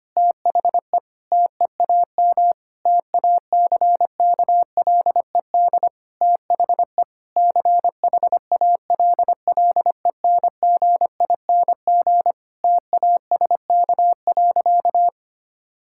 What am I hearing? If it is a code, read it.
Morse code, heard as a keyed tone, about 25 wpm, 700 Hz. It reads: THE TEAM TACKLED THE CHALLENGING TASK.
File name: practice / 25wpm / Difficult Sentences with Hints - 25wpm